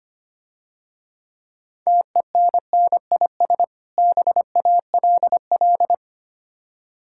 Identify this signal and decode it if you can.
Morse code, heard as a keyed tone, about 25 wpm, 690 Hz. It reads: TENNIS BALL
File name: Easy Words - Speed-Racing with Hints - 25wpm